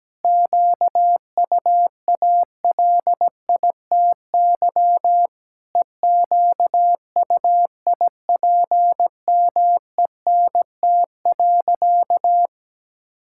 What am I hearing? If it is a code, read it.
Morse code, heard as a keyed tone, about 17 wpm, 700 Hz. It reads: QUALITY EQUIPMENT.
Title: Sentences from Top 1000 Words - Review Part 6 - 17wpm